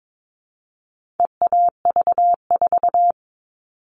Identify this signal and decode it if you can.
Morse code, heard as a keyed tone, about 22 wpm, 700 Hz. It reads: EAV4
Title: All Cut Numbers - 4-Digit 22wpm